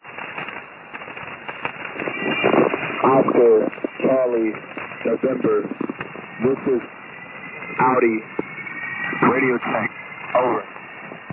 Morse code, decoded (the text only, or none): none